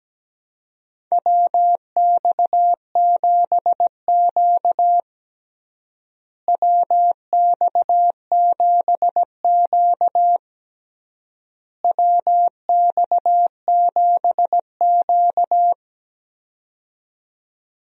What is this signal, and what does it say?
Morse code, heard as a keyed tone, about 17 wpm, 700 Hz. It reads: WX7Q WX7Q WX7Q
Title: USA Call Signs - 4 Characters - Repeated 3x - 17wpm